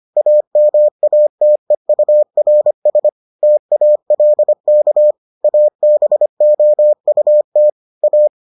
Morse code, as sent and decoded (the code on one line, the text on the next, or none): .- -- .- - . ..- .-. ... / - .- .-.. -.- / .- -... --- ..- - / .-
AMATEURS TALK ABOUT A